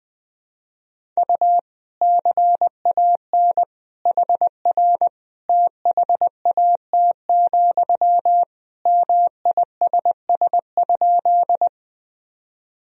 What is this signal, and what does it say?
Morse code, heard as a keyed tone, about 21 wpm, 700 Hz. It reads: U CAN HR THAT, MISS?
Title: Sentences from Top 500 Words and Abbreviations 20wpm